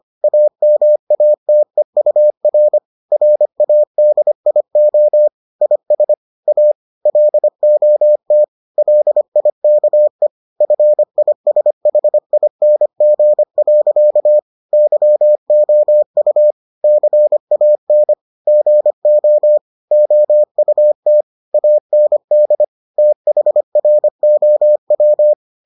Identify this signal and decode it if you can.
Morse code, heard as a keyed tone, about 25 wpm, 600 Hz. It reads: AMATEUR RADIO IS A LOT LIKE FISHING. YOU CAN GO OUT AND THROW